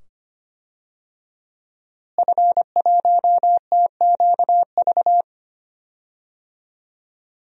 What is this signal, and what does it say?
Morse code, heard as a keyed tone, about 25 wpm, 700 Hz. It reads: F1TQV
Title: DXCC-100 Call Signs - 5 Characters - Repeated 2x - 25wpm